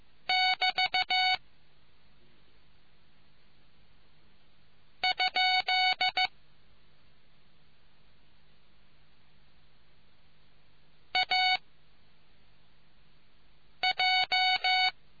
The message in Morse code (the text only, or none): <BT> ? AJ